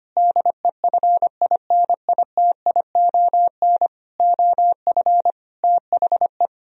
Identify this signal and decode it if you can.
Morse code, heard as a keyed tone, about 25 wpm, 700 Hz. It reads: DEFINITION OF THE
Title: Sentences from Top 1000 Words - Review Part 4 - 25wpm